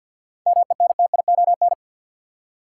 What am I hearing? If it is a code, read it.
Morse code, heard as a keyed tone, about 52 wpm, 700 Hz. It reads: MENTION
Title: Seven Letter Words - 50wpm